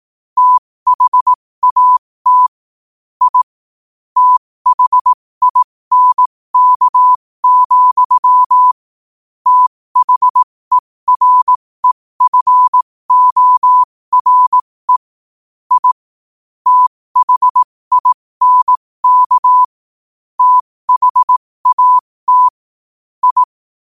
Morse code, heard as a keyed tone, about 18 wpm, spaced as for 15 wpm, 1000 Hz. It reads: THAT I THINK, THEREFORE I THINK THAT I
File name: qotdfiles / 15wpm